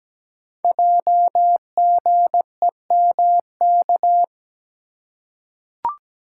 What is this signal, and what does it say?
Morse code, heard as a keyed tone, about 17 wpm, 700 Hz. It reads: JGEMK
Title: Code Groups - 5 Letters - 17wpm